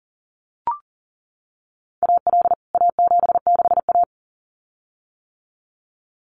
Morse code, heard as a keyed tone, about 40 wpm, 700 Hz. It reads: AL A76A
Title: CWT Exchange 40wpm